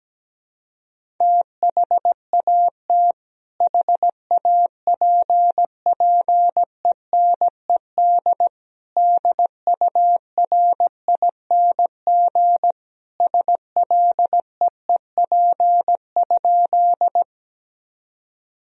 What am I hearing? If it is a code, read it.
Morse code, heard as a keyed tone, about 17 wpm, 700 Hz. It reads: THAT HAPPENED DURING SLEEP?
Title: Sentences from Top 1000 Words - Review Part 2 - 17wpm